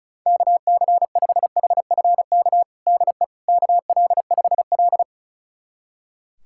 Morse code, heard as a keyed tone, about 34 wpm, 700 Hz. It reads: KC5HFK DE KL5L
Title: USA Callsign DE Callsign 35wpm